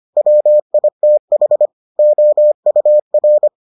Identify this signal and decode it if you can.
Morse code, heard as a keyed tone, about 25 wpm, 600 Hz. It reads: WITH OUR